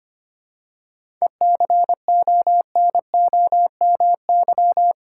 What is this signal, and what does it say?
Morse code, heard as a keyed tone, about 25 wpm, 700 Hz. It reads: ECONOMY
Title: Seven Letter Words - 25wpm